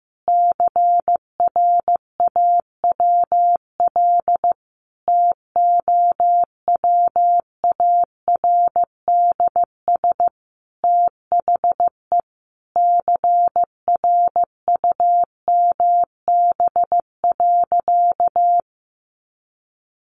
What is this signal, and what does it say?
Morse code, heard as a keyed tone, about 15 wpm, 700 Hz. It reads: CRAWL TOWARDS THE CRUMB.
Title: Easy Phrases 15wpm